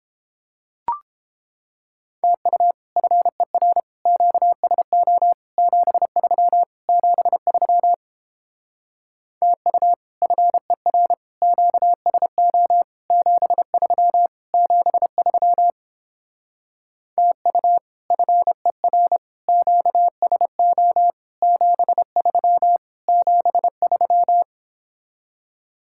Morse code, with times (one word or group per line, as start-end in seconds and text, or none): 2.2-2.7 TU
3.0-3.8 FER
4.1-5.3 QSO
5.6-6.6 73
6.9-7.9 73
9.4-9.9 TU
10.2-11.1 FER
11.4-12.8 QSO
13.1-14.3 73
14.5-15.7 73
17.2-17.8 TU
18.1-19.2 FER
19.5-21.1 QSO
21.4-22.8 73
23.1-24.4 73